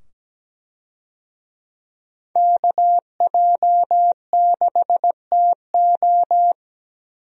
Morse code, heard as a keyed tone, about 17 wpm, 700 Hz. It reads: KJ6TO
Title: USA Call Signs - 5 Characters - 17wpm